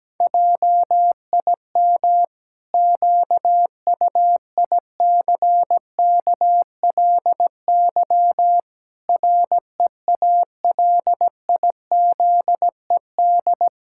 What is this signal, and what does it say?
Morse code, heard as a keyed tone, about 17 wpm, 700 Hz. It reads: JIM QUICKLY REALIZED
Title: Pangram Challenge v2 - 17wpm